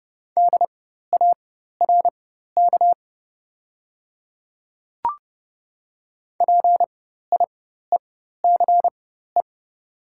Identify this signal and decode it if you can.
Morse code, heard as a keyed tone, about 29 wpm, spaced as for 14 wpm, 700 Hz. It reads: DARK PIECE
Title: 4x Character Spacing - Top 500 Words - 30wpm